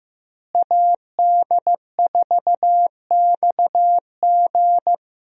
Morse code, heard as a keyed tone, about 16 wpm, 700 Hz. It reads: AD4XG
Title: USA Call Signs - 5 Characters - Repeated 2x - 15wpm